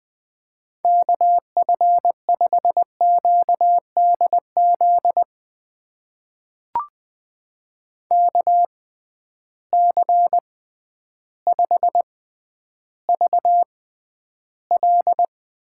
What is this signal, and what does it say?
Morse code, heard as a keyed tone, about 20 wpm, spaced as for 11 wpm, 700 Hz. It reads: KF5QDZ KC5VL
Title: USA Call Signs - 6 Characters - 6x Character Spacing - 20wpm